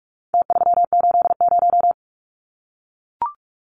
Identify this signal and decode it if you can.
Morse code, heard as a keyed tone, about 44 wpm, 700 Hz. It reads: T380
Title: Numbers - Contest Style - 4-Digit 45wpm